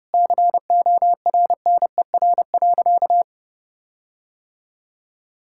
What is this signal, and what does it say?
Morse code, heard as a keyed tone, about 30 wpm, 700 Hz. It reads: CORNER.
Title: Sentences from Top 1600 Words 30wpm